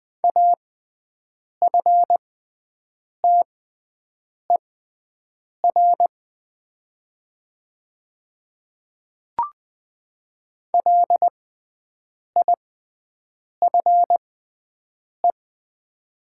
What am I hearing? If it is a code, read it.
Morse code, heard as a keyed tone, about 20 wpm, spaced as for 7 wpm, 700 Hz. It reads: AFTER LIFE
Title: 6x Character Spacing - Top 500 Words - 20wpm